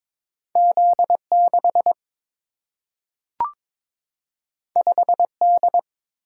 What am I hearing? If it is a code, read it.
Morse code, heard as a keyed tone, about 22 wpm, 700 Hz. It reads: Z 6 5 D
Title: Single-Letter Single-Number 22wpm